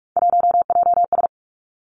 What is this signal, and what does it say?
Morse code, heard as a keyed tone, about 46 wpm, 700 Hz. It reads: 1JS
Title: Three Challenging Letter-Number Combinations 45wpm